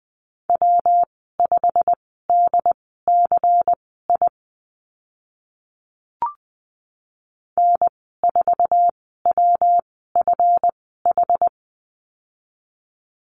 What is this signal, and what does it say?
Morse code, heard as a keyed tone, about 20 wpm, spaced as for 14 wpm, 700 Hz. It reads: W5DCI N4WFH
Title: 140 Lesson 23g - Letter f - Call Signs 20wpm